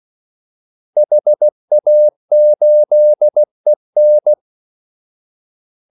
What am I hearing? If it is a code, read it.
Morse code, heard as a keyed tone, about 16 wpm, 600 Hz. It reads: HA8EN
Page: Kaldesignaler 16 wpm